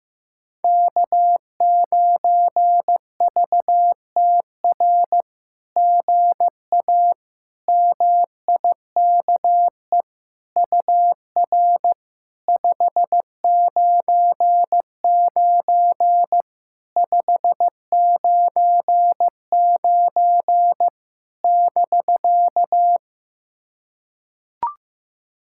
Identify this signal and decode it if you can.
Morse code, heard as a keyed tone, about 15 wpm, 700 Hz. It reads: K9VTR GA MIKE UR 599 599 <BK>
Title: SOTA QSO Practice 15wpm